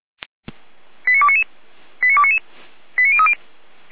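Five-tone selective call (ZVEI), heard as quick runs of five tones, tones of about 70 ms. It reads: 89299 89299 89030
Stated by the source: ZVEI five-tone call received by SDR.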